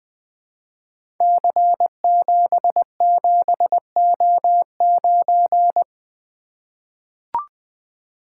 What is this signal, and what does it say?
Morse code, heard as a keyed tone, about 21 wpm, 700 Hz. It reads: C77O9
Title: Code Groups - 5 Letter:Numbers - 20wpm